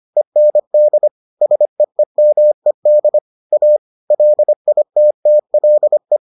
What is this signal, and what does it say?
Morse code, heard as a keyed tone, about 25 wpm, 600 Hz. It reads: END SEEMED A LITTLE